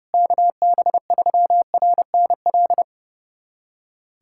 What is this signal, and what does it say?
Morse code, heard as a keyed tone, about 32 wpm, 700 Hz. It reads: KB3RNL
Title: USA Callsign DE Callsign 30wpm